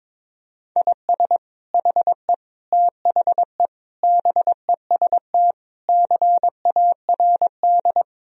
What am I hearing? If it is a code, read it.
Morse code, heard as a keyed tone, about 22 wpm, 700 Hz. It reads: IS HE THE BEST CARD
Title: Sentences from Top 1000 Words - Review Part 4 - 22wpm